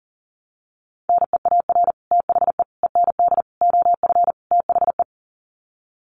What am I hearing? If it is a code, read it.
Morse code, heard as a keyed tone, about 40 wpm, 700 Hz. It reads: NEAR THE END OF THE
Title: Sets of 5 Words Encore 40wpm